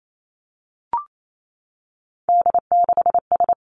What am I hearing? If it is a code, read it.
Morse code, heard as a keyed tone, about 28 wpm, 700 Hz. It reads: D6S
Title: Three Challenging Letter-Number Combinations 28wpm